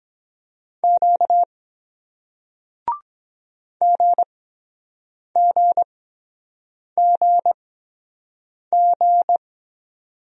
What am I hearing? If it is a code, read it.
Morse code, heard as a keyed tone, about 23 wpm, spaced as for 7 wpm, 700 Hz. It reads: Q GGGG